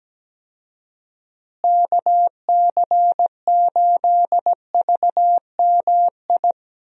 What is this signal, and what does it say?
Morse code, heard as a keyed tone, about 17 wpm, 700 Hz. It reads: KC8VMI
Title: USA Call Signs - 6 Characters - Repeated 3x - 17wpm